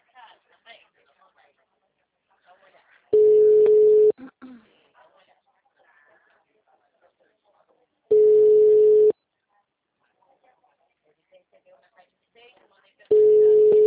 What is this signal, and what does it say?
Ringback tone (Europe, 425 Hz), with speech.
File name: out- -2017- - - .23584